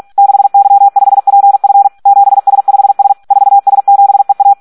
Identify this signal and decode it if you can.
Morse code, heard as a keyed tone, about 46 wpm, 800 Hz. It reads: 6Y5PF 7S5S 4S7EA
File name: Audio practica CW / SD 1 46